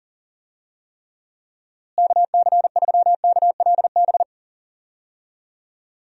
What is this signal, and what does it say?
Morse code, heard as a keyed tone, about 37 wpm, 700 Hz. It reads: KC3KLB
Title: USA Call Signs - 6 Characters - Repeated 3x - 40wpm